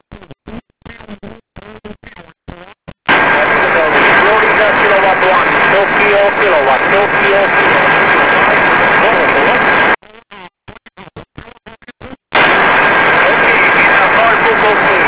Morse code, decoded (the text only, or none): none